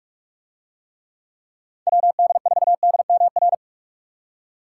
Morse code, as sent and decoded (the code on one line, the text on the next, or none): .-- -.. ....- -.. -- .-.
WD4DMR